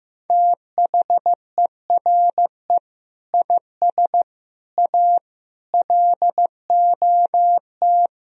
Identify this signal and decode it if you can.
Morse code, heard as a keyed tone, about 15 wpm, 700 Hz. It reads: THERE IS A LOT
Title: Sentences from Top 1000 Words - Review Part 4 - 15wpm